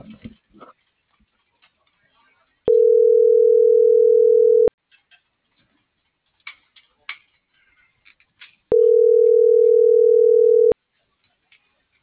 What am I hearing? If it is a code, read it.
Ringback tone (North America), 440 + 480 Hz.